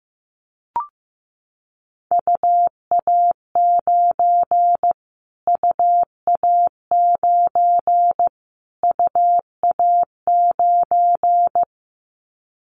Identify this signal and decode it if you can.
Morse code, heard as a keyed tone, about 15 wpm, 700 Hz. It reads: UA9 UA9 UA9
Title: Callsign Prefixes for Top 25 DXCC - Repeated 3x - 15wpm